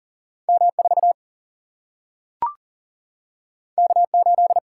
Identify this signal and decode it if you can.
Morse code, heard as a keyed tone, about 40 wpm, 700 Hz. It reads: M 4 K 8
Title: Single-Letter Single-Number 40wpm